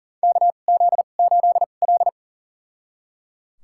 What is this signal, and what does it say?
Morse code, heard as a keyed tone, about 37 wpm, spaced as for 28 wpm, 690 Hz. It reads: KZ8L
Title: USA Call Signs - 4 Characters - 2x Character Spacing - 40wpm